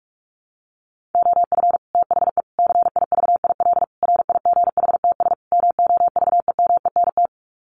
Morse code, heard as a keyed tone, about 44 wpm, 700 Hz. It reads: OF THE CIVIL RIGHTS MOVEMENT
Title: Sets of 5 Words Encore2 45wpm